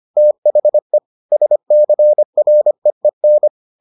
Morse code, heard as a keyed tone, about 25 wpm, 600 Hz. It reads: THE SCREEN